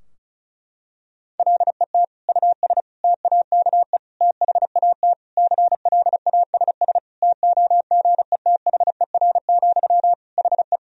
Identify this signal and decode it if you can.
Morse code, heard as a keyed tone, about 34 wpm, 700 Hz. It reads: LET US TAKE THAT CLASS TOGETHER, HE
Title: Sentences from Top 1000 Words - Review Part 2 - 35wpm